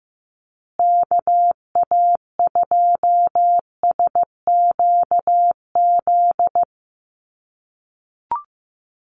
Morse code, heard as a keyed tone, about 15 wpm, 700 Hz. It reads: KA2SQZ
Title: USA Call Signs Repeated 2x 15wpm